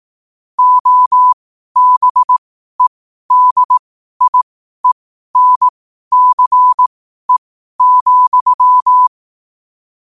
Morse code, heard as a keyed tone, about 18 wpm, spaced as for 12 wpm, 1000 Hz. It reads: OBEDIENCE,
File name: qotdfiles / 12wpm